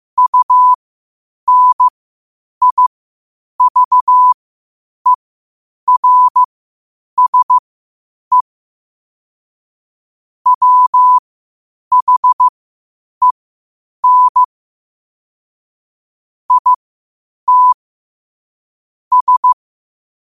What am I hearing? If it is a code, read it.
Morse code, heard as a keyed tone, about 15 wpm, spaced as for 8 wpm, 1000 Hz. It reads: UNIVERSE WHEN IT S
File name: qotdfiles / 8wpm